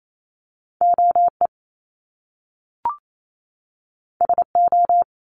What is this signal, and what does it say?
Morse code, heard as a keyed tone, about 28 wpm, 700 Hz. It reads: OE SO
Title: Top Two Letter Combinations in Words 28wpm